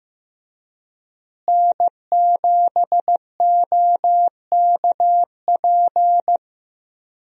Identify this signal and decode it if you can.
Morse code, heard as a keyed tone, about 15 wpm, 700 Hz. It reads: N7OKP
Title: USA Call Signs - 5 Characters - Repeated 2x - 15wpm